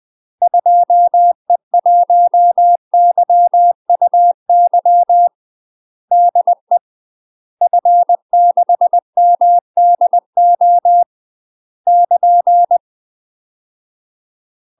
Morse code, heard as a keyed tone, about 20 wpm, 700 Hz. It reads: 2E1YUY DE F6MDO <KN>
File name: UK MASTER QSOs TRAINING / 20WPM / UK MASTER QSOs 0015